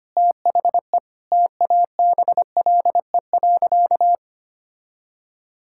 Morse code, heard as a keyed tone, about 25 wpm, 700 Hz. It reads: THE TABLE.
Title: Sentences from Top 1000 Words - Review Part 6 - 25wpm